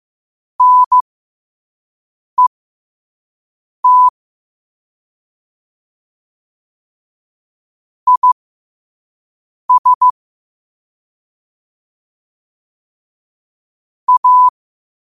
Morse code, heard as a keyed tone, about 15 wpm, spaced as for 5 wpm, 1000 Hz. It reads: NET IS A
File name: 2026-02-15-5wpm